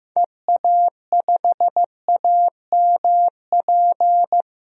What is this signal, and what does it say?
Morse code, heard as a keyed tone, about 15 wpm, 700 Hz. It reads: EA5AMP
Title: DXCC-100 Call Signs - 6 Characters - 15wpm